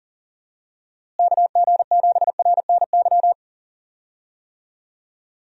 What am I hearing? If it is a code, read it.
Morse code, heard as a keyed tone, about 40 wpm, 700 Hz. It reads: KG7RNY
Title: USA Call Signs - 6 Characters - Repeated 3x - 40wpm